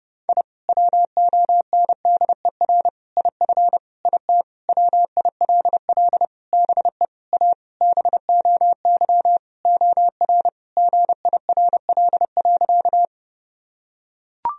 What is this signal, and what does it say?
Morse code, heard as a keyed tone, about 30 wpm, 700 Hz. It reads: I WONDER IF IT WILL BE A BOY OR GIRL.
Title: Sentences from Top 1000 Words - Review Part 3 - 30wpm